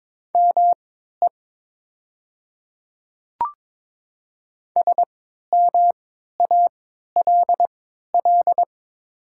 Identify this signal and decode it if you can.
Morse code, heard as a keyed tone, about 22 wpm, spaced as for 12 wpm, 700 Hz. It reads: ME SMALL
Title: 3x Character Spacing - Top 300 Words - 22wpm